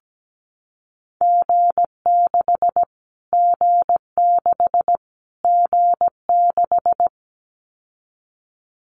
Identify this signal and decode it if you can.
Morse code, heard as a keyed tone, about 17 wpm, 700 Hz. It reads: G6 G6 G6
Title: Callsign Prefixes for Top 25 DXCC - Repeated 3x - 17wpm